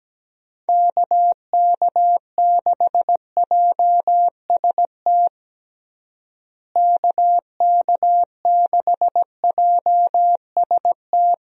Morse code, heard as a keyed tone, about 17 wpm, 700 Hz. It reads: KK6JST KK6JST
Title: USA Call Signs - 6 Characters - Repeated 2x - 17wpm